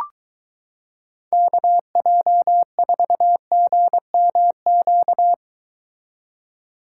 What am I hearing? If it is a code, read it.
Morse code, heard as a keyed tone, about 23 wpm, 700 Hz. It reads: KJ4GMQ